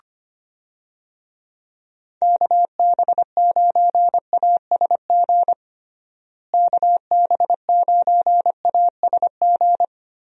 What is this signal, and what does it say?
Morse code, heard as a keyed tone, about 25 wpm, 700 Hz. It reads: KB9ASG KB9ASG
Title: USA Call Signs - 6 Characters - Repeated 3x - 25wpm